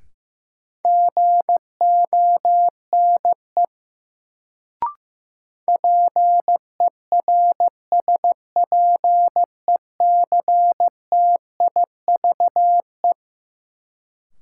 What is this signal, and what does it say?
Morse code, heard as a keyed tone, about 15 wpm, 700 Hz. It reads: GONE PERSPECTIVE
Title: Sentences from Top 1200 Words - New Word Drill - 15wpm